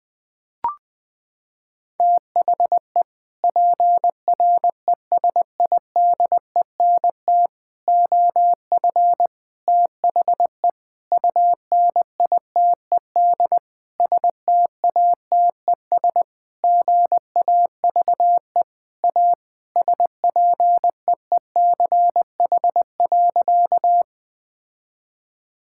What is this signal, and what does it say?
Morse code, heard as a keyed tone, about 20 wpm, 700 Hz. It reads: THE PRESIDENT OF THE UNITED STATES GAVE A SPEECH.
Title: Sentences with 5-Word Phrases - 20wpm